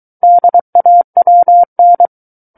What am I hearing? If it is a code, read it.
Morse code, heard as a keyed tone, about 23 wpm, 700 Hz. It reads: DAWN
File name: Book One / 14-Chapter 14 - 23 WPM